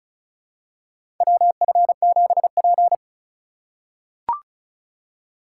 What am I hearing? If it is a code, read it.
Morse code, heard as a keyed tone, about 36 wpm, 700 Hz. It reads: WF7P
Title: USA Call Signs - 4 Characters - 2x Character Spacing - 35wpm